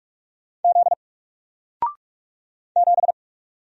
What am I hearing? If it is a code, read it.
Morse code, heard as a keyed tone, about 45 wpm, 700 Hz. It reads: Z 7
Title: Single Letter-Number 45wpm